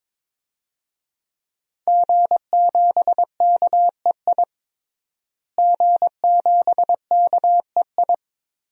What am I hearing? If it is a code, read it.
Morse code, heard as a keyed tone, about 21 wpm, 700 Hz. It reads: G7KEI G7KEI
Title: DXCC-100 Call Signs - 5 Characters - Repeated 3x - 22wpm